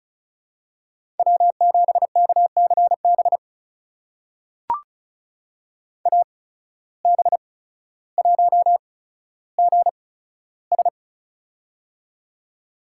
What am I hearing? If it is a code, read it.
Morse code, heard as a keyed tone, about 36 wpm, spaced as for 15 wpm, 700 Hz. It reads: W7KCB AB1GS
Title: USA Call Signs - 8x Character Spacing - 35wpm